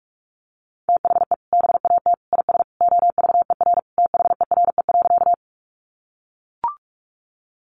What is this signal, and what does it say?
Morse code, heard as a keyed tone, about 44 wpm, 700 Hz. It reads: THE BAT IS OVER THERE.
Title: Sentences from Top 1500 Words 45wpm